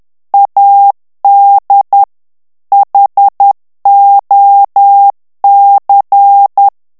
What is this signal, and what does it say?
Morse code, heard as a keyed tone, about 10 wpm, 800 Hz. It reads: AD HOC
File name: ANGLAIS texte clair 3 vitesse 10 mots minute